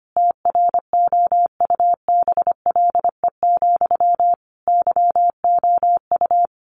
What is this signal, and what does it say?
Morse code, heard as a keyed tone, about 25 wpm, 700 Hz. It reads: TROUBLE, YOU
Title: Sentences from Top 1200 Words 25wpm